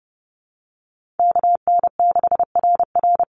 Morse code, heard as a keyed tone, about 30 wpm, 700 Hz. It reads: KN6RR
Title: USA Call Signs 30wpm